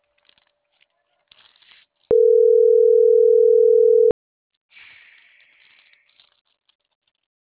Ringback tone (North America), 440 + 480 Hz.